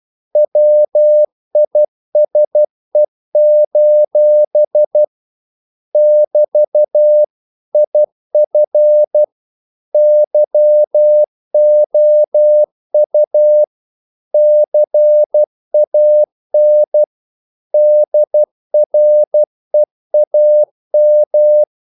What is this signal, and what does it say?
Morse code, heard as a keyed tone, about 12 wpm, 600 Hz. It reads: WISE: <BT> IF YOU CAN DREAM
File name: CW / Practice / If12